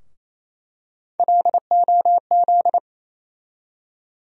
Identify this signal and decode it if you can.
Morse code, heard as a keyed tone, about 28 wpm, 700 Hz. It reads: LOZ
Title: USA Call Sign Postfixes - 3 Letters 28wpm